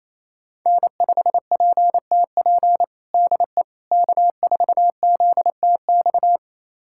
Morse code, heard as a keyed tone, about 27 wpm, 700 Hz. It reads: N5PTP DE K4ZTX
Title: USA Callsign DE Callsign - 5 Characters - 28wpm